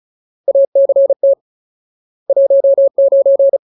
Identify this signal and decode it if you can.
Morse code, heard as a keyed tone, about 34 wpm, 550 Hz. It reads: ACT 19
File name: Ultimate Word List / 35WPM Ultimate